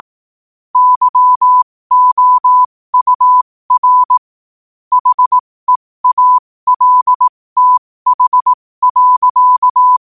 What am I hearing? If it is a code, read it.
Morse code, heard as a keyed tone, about 18 wpm, spaced as for 15 wpm, 1000 Hz. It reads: YOUR HEALTH.
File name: qotdfiles / 15wpm